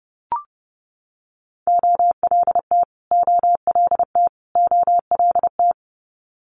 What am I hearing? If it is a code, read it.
Morse code, heard as a keyed tone, about 30 wpm, 700 Hz. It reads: OLT OLT OLT
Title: Multi-letter Phonemes Repeated 3x 30wpm